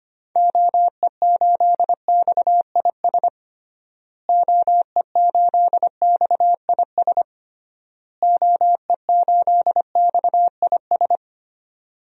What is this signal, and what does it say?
Morse code, heard as a keyed tone, about 25 wpm, 700 Hz. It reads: OE8XIS OE8XIS OE8XIS
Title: DXCC-100 Call Signs - 6 Characters - Repeated 3x - 25wpm